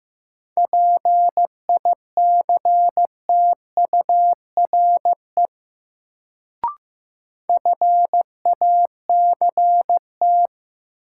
Morse code, heard as a keyed tone, about 15 wpm, 700 Hz. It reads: PICTURE FACT
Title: Sentences from Top 500 Words - New Word Drill - 15wpm